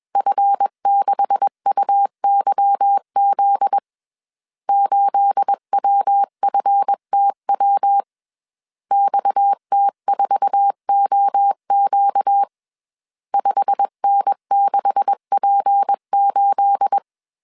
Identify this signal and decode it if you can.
Morse code, heard as a keyed tone, about 21 wpm, 800 Hz. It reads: F6UYZ 8WFTW XT4OQ 5N6P8